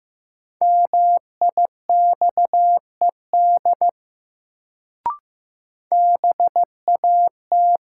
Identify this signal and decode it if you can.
Morse code, heard as a keyed tone, about 15 wpm, 700 Hz. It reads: MIXED BAT
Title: Sentences from Top 1500 Words - New Word Drill Alt - 15wpm